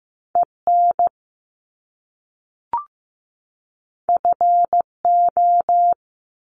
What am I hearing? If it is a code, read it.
Morse code, heard as a keyed tone, about 15 wpm, 700 Hz. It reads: EN FO
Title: Top Two Letter Combinations in Words 15wpm